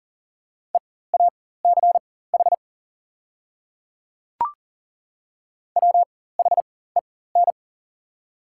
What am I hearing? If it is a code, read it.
Morse code, heard as a keyed tone, about 40 wpm, spaced as for 19 wpm, 690 Hz. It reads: EACH WHEN
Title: 4x Character Spacing - Top 100 Words - 40wpm